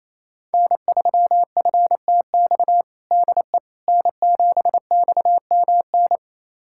Morse code, heard as a keyed tone, about 27 wpm, 700 Hz. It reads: N3FTX DE N7XMN
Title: USA Callsign DE Callsign - 5 Characters - 28wpm